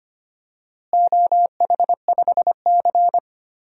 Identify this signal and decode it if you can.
Morse code, heard as a keyed tone, about 25 wpm, 700 Hz. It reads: OH5C
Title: DXCC-100 Call Signs - 4 Characters - Repeated 2x - 25wpm